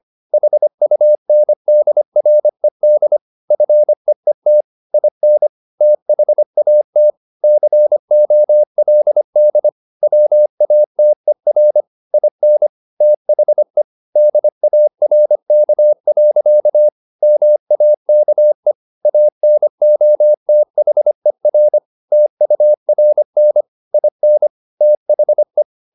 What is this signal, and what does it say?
Morse code, heard as a keyed tone, about 25 wpm, 600 Hz. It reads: HUNDRED FEET IN THAT COLD WATER IN THE DARK. MAKE ANOTHER TURN IN THE
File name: the old man and the sea part 1 25wpm